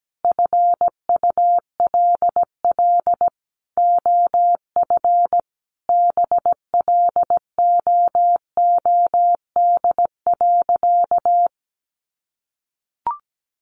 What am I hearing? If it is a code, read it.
Morse code, heard as a keyed tone, about 17 wpm, 700 Hz. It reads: FULL OF BLOOD.
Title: Sentences from Top 1100 Words 17wpm